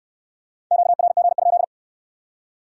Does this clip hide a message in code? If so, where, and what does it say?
Yes: Morse code from 0.7 to 1.6 s, heard as a keyed tone, about 63 wpm, 700 Hz. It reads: QRG?